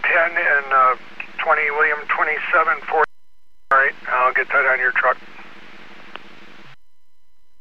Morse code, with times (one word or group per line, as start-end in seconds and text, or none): none